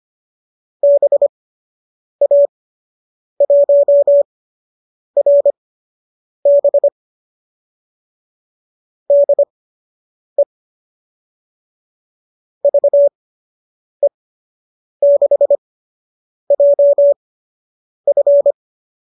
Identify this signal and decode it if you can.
Morse code, heard as a keyed tone, about 25 wpm, spaced as for 8 wpm, 570 Hz. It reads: BA1RB DE VE6JF